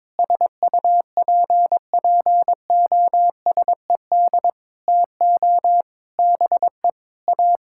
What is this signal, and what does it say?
Morse code, heard as a keyed tone, about 22 wpm, 700 Hz. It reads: SUPPOSED TO BE A
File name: practice / 22wpm / Sentences with 4-Word Phrases - 22wpm